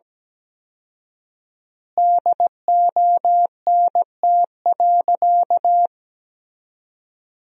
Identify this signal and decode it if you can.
Morse code, heard as a keyed tone, about 17 wpm, 700 Hz. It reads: DONT.
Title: 4x Word-Spacing - Sentences from Top 300 Words - 17wpm